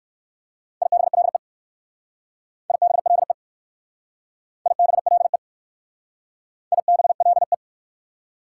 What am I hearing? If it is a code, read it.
Morse code, heard as a keyed tone, about 60 wpm, 690 Hz. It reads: IBLE IBLE IBLE IBLE